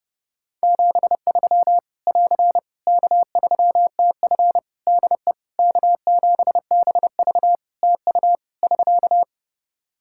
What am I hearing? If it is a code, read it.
Morse code, heard as a keyed tone, about 30 wpm, 700 Hz. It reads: 73 <AR> K3TF DE K7BV TU <SK>
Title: Practice QSO - Exchange 4 - Ending - 30wpm